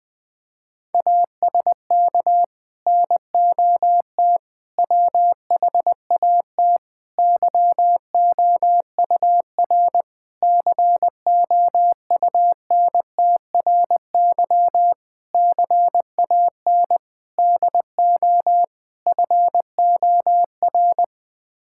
Morse code, heard as a keyed tone, about 20 wpm, 700 Hz. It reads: ASK NOT WHAT YOUR COUNTRY CAN DO FOR
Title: Sets of 8 Words - 20wpm